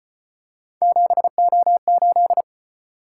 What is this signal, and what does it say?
Morse code, heard as a keyed tone, about 35 wpm, 700 Hz. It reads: 7O8